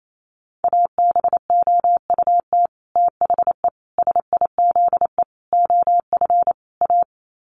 Morse code, heard as a keyed tone, about 28 wpm, 700 Hz. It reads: ABOUT THE SIZE OF A
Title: Sets of 5 Words 28wpm